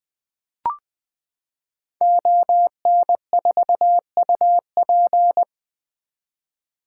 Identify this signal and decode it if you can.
Morse code, heard as a keyed tone, about 20 wpm, 700 Hz. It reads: ON4UP
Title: 2023 SOTA Activator Call Signs - 20wpm